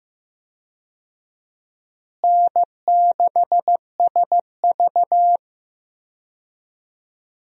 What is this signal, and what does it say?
Morse code, heard as a keyed tone, about 15 wpm, 700 Hz. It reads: N6SV
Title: USA Call Signs - 4 Characters - Repeated 3x - 15wpm